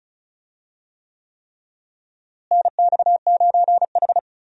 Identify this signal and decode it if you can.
Morse code, heard as a keyed tone, about 34 wpm, 700 Hz. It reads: NX9H
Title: USA Call Signs - 4 Characters - Repeated 3x - 35wpm